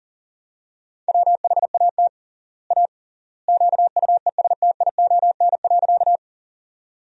Morse code, heard as a keyed tone, about 40 wpm, 700 Hz. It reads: WHAT A QUESTION.
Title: 3x Word-Spacing - Sentences from Top 400 Words - 40wpm